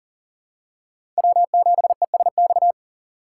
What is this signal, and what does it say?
Morse code, heard as a keyed tone, about 40 wpm, 700 Hz. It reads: W7ESX
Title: USA Call Signs - 6x Character Spacing - 40wpm